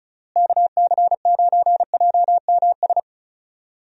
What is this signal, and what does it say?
Morse code, heard as a keyed tone, about 34 wpm, 700 Hz. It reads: KC9JMS
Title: USA Call Signs - 6 Characters - Repeated 2x - 35wpm